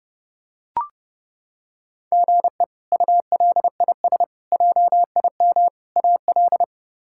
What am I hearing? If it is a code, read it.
Morse code, heard as a keyed tone, about 30 wpm, 700 Hz. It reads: GE ULIS JIM AL
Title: K1USN SST QSO Practice - 30wpm